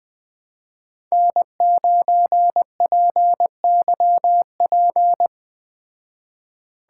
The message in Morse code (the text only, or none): N9PYP